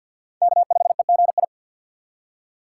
Morse code, heard as a keyed tone, about 47 wpm, 700 Hz. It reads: K5EGI
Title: USA Call Signs - 5 Characters - 3x Character Spacing - 50wpm